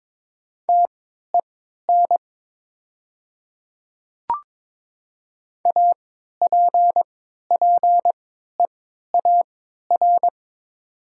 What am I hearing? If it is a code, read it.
Morse code, heard as a keyed tone, about 22 wpm, spaced as for 13 wpm, 700 Hz. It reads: TEN APPEAR
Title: 3x Character Spacing - Top 500 Words - 22wpm